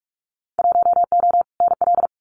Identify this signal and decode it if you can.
Morse code, heard as a keyed tone, about 46 wpm, 700 Hz. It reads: 1O NL
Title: Winter Field Day Exchanges Encore 45wpm